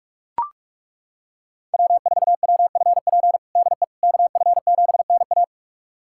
Morse code, heard as a keyed tone, about 42 wpm, 700 Hz. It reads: W4WFP DE KF7NA
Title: USA Callsign DE Callsign - 5 Characters - 45wpm